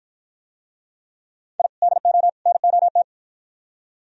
Morse code, heard as a keyed tone, about 52 wpm, 700 Hz. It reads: I DO NOT